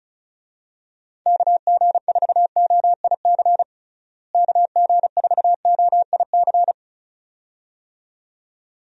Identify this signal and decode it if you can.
Morse code, heard as a keyed tone, about 36 wpm, 700 Hz. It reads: KG4OIC KG4OIC
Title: USA Call Signs - 6 Characters - Repeated 2x - 35wpm